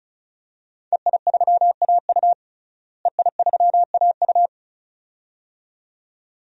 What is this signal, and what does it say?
Morse code, heard as a keyed tone, about 36 wpm, 700 Hz. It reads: EI3AU EI3AU
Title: DXCC-100 Call Signs - 5 Characters - Repeated 2x - 35wpm